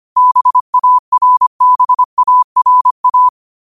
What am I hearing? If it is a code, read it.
Morse code, heard as a keyed tone, about 25 wpm, 1000 Hz. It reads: BARBARA
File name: qotdfiles / 25wpm